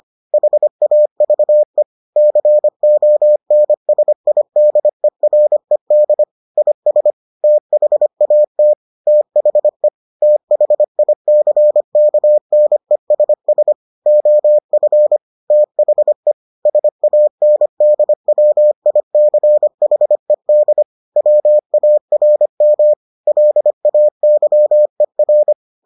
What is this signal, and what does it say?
Morse code, heard as a keyed tone, about 25 wpm, 600 Hz. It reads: HAVE CONSIDERED IS THAT THE THICKNESS OF THE SANDWICHED WARM LAYER